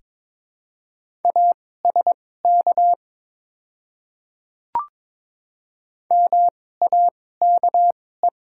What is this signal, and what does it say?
Morse code, heard as a keyed tone, about 22 wpm, spaced as for 16 wpm, 690 Hz. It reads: ASK MAKE
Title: 2x Character Spacing - Top 500 Words - 22wpm